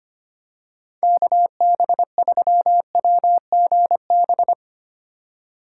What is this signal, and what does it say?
Morse code, heard as a keyed tone, about 25 wpm, 700 Hz. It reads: KB3WGB
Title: USA Call Signs - 6 Characters - 3x Character Spacing - 25wpm